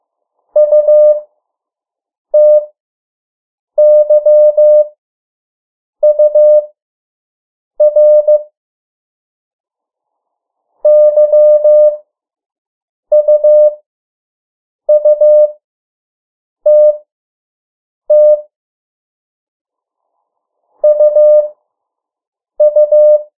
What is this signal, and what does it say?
Morse code, heard as a keyed tone, about 15 wpm, spaced as for 6 wpm, 600 Hz. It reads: UTYUR YUUTT UU
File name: morse mp3 / 03 urok 2